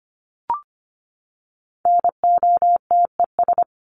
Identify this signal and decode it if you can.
Morse code, heard as a keyed tone, about 25 wpm, 700 Hz. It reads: NOTES
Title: ES:S Suffix - Top 100 Words with Base Words 25wpm